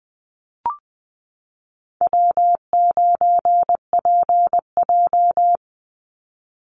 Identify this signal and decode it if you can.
Morse code, heard as a keyed tone, about 20 wpm, 700 Hz. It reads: W9PJ
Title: Easy Call Signs Repeated 2x - 20wpm